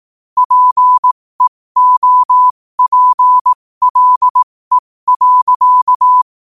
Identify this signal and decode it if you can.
Morse code, heard as a keyed tone, about 18 wpm, spaced as for 15 wpm, 1000 Hz. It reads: PEOPLE.
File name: qotdfiles / 15wpm